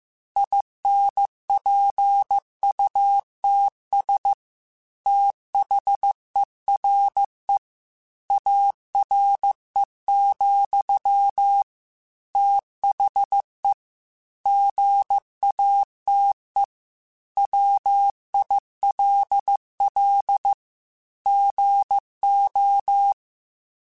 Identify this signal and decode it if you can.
Morse code, heard as a keyed tone, about 15 wpm, 800 Hz. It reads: INPUTS THERE ARE, THE GATE WILL GO
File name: Technical Subjects / Technical Subjects-mp3s / OR Gate @13